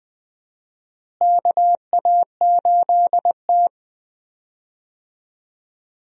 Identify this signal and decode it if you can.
Morse code, heard as a keyed tone, about 20 wpm, 700 Hz. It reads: KA8T
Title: USA Call Signs - 4 Characters - Repeated 3x - 20wpm